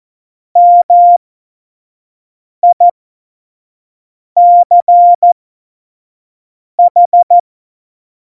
Morse code, heard as a keyed tone, about 14 wpm, spaced as for 5 wpm, 700 Hz. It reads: MICH
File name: Morsekurs / Witze / Witz 06. November 2018, 25 F70